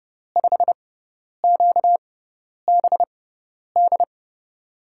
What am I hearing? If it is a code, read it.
Morse code, heard as a keyed tone, about 30 wpm, spaced as for 10 wpm, 700 Hz. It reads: 5QBD
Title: USA Call Signs - 6 Characters - 6x Character Spacing - 30wpm